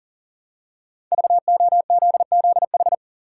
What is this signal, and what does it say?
Morse code, heard as a keyed tone, about 37 wpm, 700 Hz. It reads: VOZZH
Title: Code Groups - 5 Letter:Numbers - 40wpm